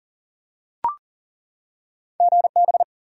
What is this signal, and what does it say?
Morse code, heard as a keyed tone, about 40 wpm, 700 Hz. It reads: GB
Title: Top 90 QSO Elements 40wpm